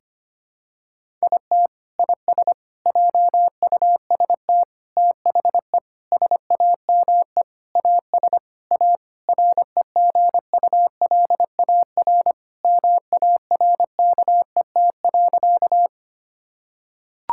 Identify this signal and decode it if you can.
Morse code, heard as a keyed tone, about 25 wpm, 700 Hz. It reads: IT IS JUST THE SAME AS A REGULAR MARKET.
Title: Sentences from Top 1000 Words - Review Part 6 - 25wpm